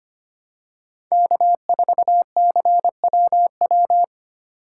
Morse code, heard as a keyed tone, about 25 wpm, 700 Hz. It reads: K4CWW
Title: USA Call Signs - 5 Characters - 2x Character Spacing - 25wpm